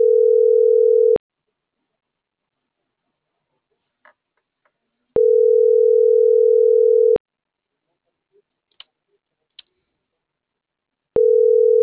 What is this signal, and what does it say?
Ringback tone (North America), 440 + 480 Hz.